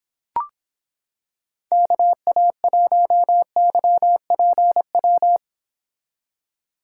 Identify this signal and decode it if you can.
Morse code, heard as a keyed tone, about 26 wpm, 700 Hz. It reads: KA1YPW